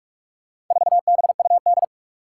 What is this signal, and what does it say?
Morse code, heard as a keyed tone, about 45 wpm, 700 Hz. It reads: 4BUD
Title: Four Challenging Letter-Number Combinations 45wpm